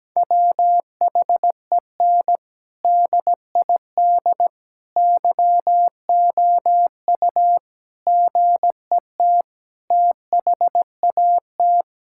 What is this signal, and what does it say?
Morse code, heard as a keyed tone, about 17 wpm, 700 Hz. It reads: WHEN DID YOU GET THAT
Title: Sentences from Top 1000 Words - Review Part 6 - 17wpm